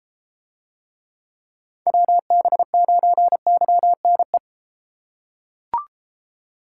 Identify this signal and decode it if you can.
Morse code, heard as a keyed tone, about 30 wpm, 700 Hz. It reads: WB9YNE